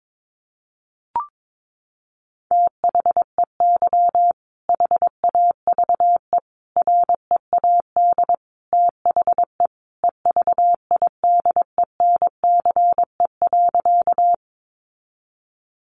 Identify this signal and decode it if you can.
Morse code, heard as a keyed tone, about 22 wpm, 700 Hz. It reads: THEY HAVE READ THE EVIDENCE.
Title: Sentences from Top 1300 Words 22wpm